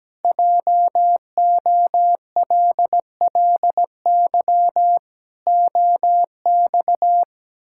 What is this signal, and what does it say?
Morse code, heard as a keyed tone, about 17 wpm, 700 Hz. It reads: JOLLY OX
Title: Pangram Challenge v2 - 17wpm